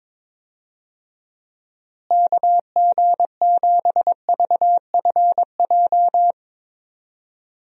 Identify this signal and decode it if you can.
Morse code, heard as a keyed tone, about 22 wpm, 700 Hz. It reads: KG7VFJ
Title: USA Call Signs - 6 Characters - Repeated 3x - 22wpm